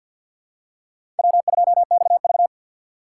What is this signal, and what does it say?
Morse code, heard as a keyed tone, about 53 wpm, 700 Hz. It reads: W2XV
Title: USA Call Signs - 4x Character Spacing - 50wpm